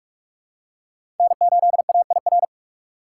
Morse code, heard as a keyed tone, about 42 wpm, 690 Hz. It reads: N8AIR
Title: USA Call Signs - 5 Characters - 8x Character Spacing - 45wpm